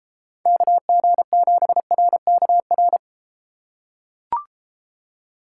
Morse code, heard as a keyed tone, about 33 wpm, 700 Hz. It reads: KG7RKR